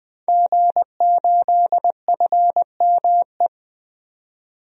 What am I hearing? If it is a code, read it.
Morse code, heard as a keyed tone, about 20 wpm, 700 Hz. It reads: G8FME
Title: DXCC-100 Call Signs - 5 Characters - 20wpm